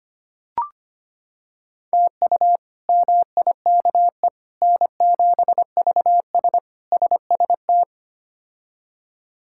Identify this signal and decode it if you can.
Morse code, heard as a keyed tone, about 25 wpm, 700 Hz. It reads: TU MIKE N7VS SST
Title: K1USN SST QSO Practice - 25wpm